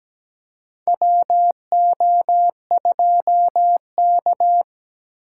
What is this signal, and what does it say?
Morse code, heard as a keyed tone, about 17 wpm, 700 Hz. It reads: WO2K
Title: USA Call Signs - 4 Characters - 3x Character Spacing - 17wpm